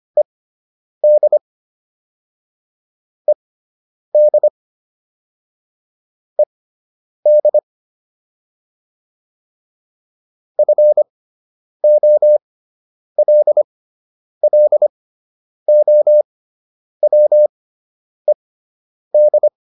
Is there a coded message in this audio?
Yes: Morse code from 0.2 to 19.5 s, heard as a keyed tone, about 25 wpm, spaced as for 9 wpm, 600 Hz. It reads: ED ED ED FOLLOWED